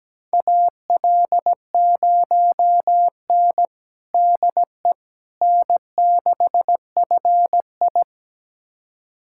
Morse code, heard as a keyed tone, about 17 wpm, 700 Hz. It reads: AL0N DE N6FI
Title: USA Callsign DE Callsign - 4 Characters - 17wpm